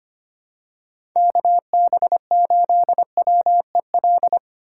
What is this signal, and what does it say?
Morse code, heard as a keyed tone, about 25 wpm, 700 Hz. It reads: KB8WEL
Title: USA Call Signs - 2x Character Spacing - 25wpm